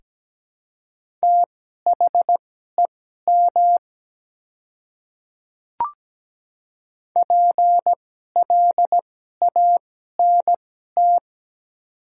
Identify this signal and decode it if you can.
Morse code, heard as a keyed tone, about 17 wpm, spaced as for 12 wpm, 700 Hz. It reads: THEM PLANT
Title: 2x Character Spacing - Top 300 Words - 17wpm